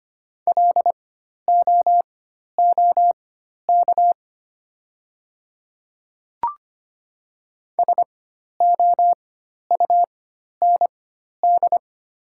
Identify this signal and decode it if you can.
Morse code, heard as a keyed tone, about 25 wpm, spaced as for 12 wpm, 700 Hz. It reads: LOOK SOUND
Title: 4x Character Spacing - Top 100 Words - 25wpm